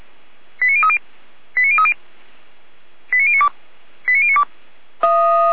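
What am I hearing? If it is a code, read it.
Five-tone selective call (ZVEI), heard as quick runs of five tones, tones of about 70 ms. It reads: 89030 89030 89092 89092
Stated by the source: Five-tone ZVEI selective call received by SDR.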